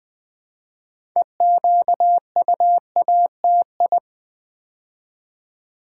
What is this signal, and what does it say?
Morse code, heard as a keyed tone, about 20 wpm, 700 Hz. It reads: EQUATI
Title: Build A Word - 2 Letters At A Time - 8-Letter Words - 20wpm